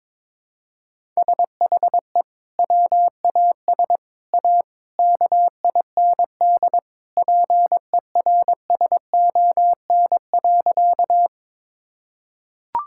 Morse code, heard as a keyed tone, about 22 wpm, 700 Hz. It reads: SHE WAS A KIND PERSON.
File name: practice / 22wpm / Sentences with 3-Word Phrases - 22wpm